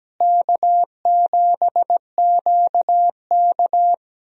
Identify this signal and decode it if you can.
Morse code, heard as a keyed tone, about 17 wpm, 700 Hz. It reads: K7QK
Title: USA Call Signs - 4 Characters - Repeated 2x - 17wpm